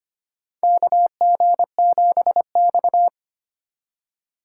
Morse code, heard as a keyed tone, about 25 wpm, 700 Hz. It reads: KG7X
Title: USA Call Signs - 4 Characters - 2x Character Spacing - 25wpm